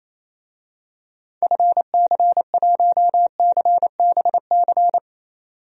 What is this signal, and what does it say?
Morse code, heard as a keyed tone, about 28 wpm, 700 Hz. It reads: FC1CBC
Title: DXCC-100 Call Signs - 6 Characters - 28wpm